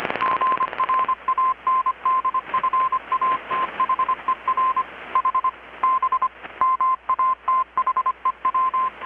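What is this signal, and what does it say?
Morse code, heard as a keyed tone, about 25 wpm, 1040 Hz. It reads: GRANDFATHER H B MATHEW
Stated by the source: Morse from KPH's Night of Nights XX maritime radiotelegraphy event, received in Chiapas, Mexico about 3,800 km away on a Kenwood R-600 with a grounded 10 m long-wire antenna.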